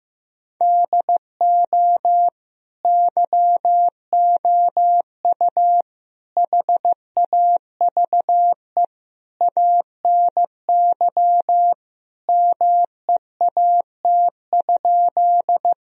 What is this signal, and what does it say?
Morse code, heard as a keyed tone, about 15 wpm, 700 Hz. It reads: DO YOU HAVE ANY MEAT?
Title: Sentences from Top 1000 Words - Review Part 3 - 15wpm